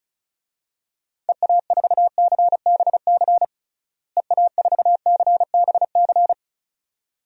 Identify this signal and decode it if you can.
Morse code, heard as a keyed tone, about 34 wpm, 700 Hz. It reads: EA4CBC EA4CBC
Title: DXCC-100 Call Signs - 6 Characters - Repeated 2x - 35wpm